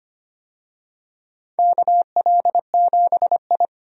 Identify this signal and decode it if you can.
Morse code, heard as a keyed tone, about 25 wpm, 700 Hz. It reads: KL7I
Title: DXCC-100 Call Signs - 4 Characters - Repeated 3x - 25wpm